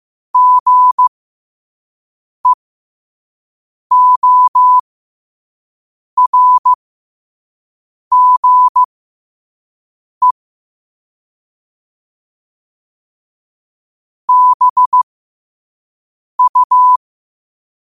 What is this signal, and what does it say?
Morse code, heard as a keyed tone, about 15 wpm, spaced as for 5 wpm, 1000 Hz. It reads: GEORGE BU
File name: qotdfiles / 5wpm